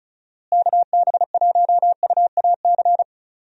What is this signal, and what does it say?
Morse code, heard as a keyed tone, about 36 wpm, 700 Hz. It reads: KB1UAC
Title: USA Call Signs - 6 Characters - Repeated 3x - 35wpm